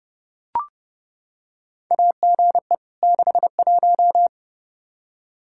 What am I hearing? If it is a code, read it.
Morse code, heard as a keyed tone, about 30 wpm, 700 Hz. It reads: AGE 61
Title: K1USN SST QSO Practice 30wpm